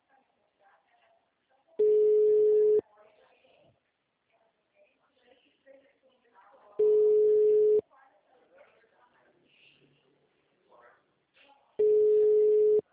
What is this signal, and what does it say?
Ringback tone (Europe, 425 Hz), with speech.
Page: Arapongas - PR